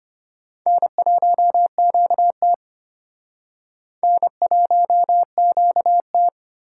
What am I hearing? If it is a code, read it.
Morse code, heard as a keyed tone, about 33 wpm, 700 Hz. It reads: N1QT N1QT